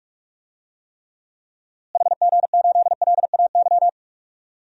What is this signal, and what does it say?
Morse code, heard as a keyed tone, about 44 wpm, 700 Hz. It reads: HG8LAY
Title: DXCC-100 Call Signs - 6 Characters - Repeated 2x - 45wpm